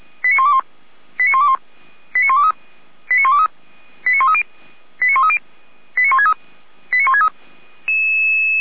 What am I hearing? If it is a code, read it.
Five-tone selective call (ZVEI), heard as quick runs of five tones, tones of about 70 ms. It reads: 89121 89121 89123 89123 89130 89130 89163 89163